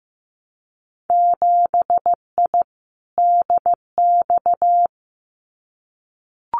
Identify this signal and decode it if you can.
Morse code, heard as a keyed tone, about 15 wpm, 700 Hz. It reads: 7I DX
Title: Winter Field Day Exchanges Encore 15wpm